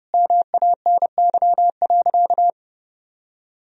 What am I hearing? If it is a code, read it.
Morse code, heard as a keyed tone, about 30 wpm, 700 Hz. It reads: MANY.
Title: Sentences from Top 1000 Words - Review Part 1 - 30wpm